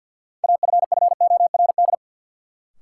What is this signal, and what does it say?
Morse code, heard as a keyed tone, about 50 wpm, 700 Hz. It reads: AFFORD
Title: Sentences from Top 1100 Words - New Word Drill - 50wpm